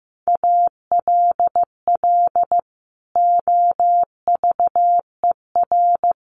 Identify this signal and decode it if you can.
Morse code, heard as a keyed tone, about 15 wpm, 700 Hz. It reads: ALL OVER
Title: Sets of 4 Words 15wpm